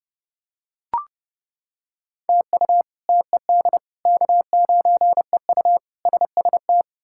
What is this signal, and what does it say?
Morse code, heard as a keyed tone, about 30 wpm, 700 Hz. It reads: TU TED K9EU SST
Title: K1USN SST QSO Practice - 30wpm